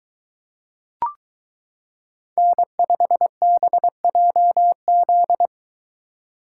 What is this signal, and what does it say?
Morse code, heard as a keyed tone, about 23 wpm, 700 Hz. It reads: N5BJZ